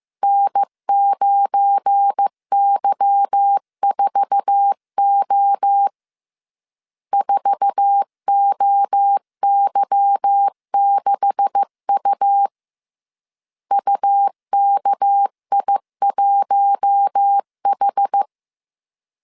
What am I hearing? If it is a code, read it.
Morse code, heard as a keyed tone, about 15 wpm, 800 Hz. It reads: N9Y4O 4OY6U UKI1H
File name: CW / DET / 1 9 1500C S A 14wpm 3222